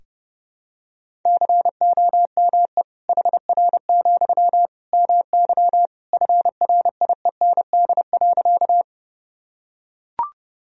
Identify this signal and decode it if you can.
Morse code, heard as a keyed tone, about 32 wpm, 700 Hz. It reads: COME HR, MY FRIEND.
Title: Sentences from Top 300 Words and Abbreviations 30wpm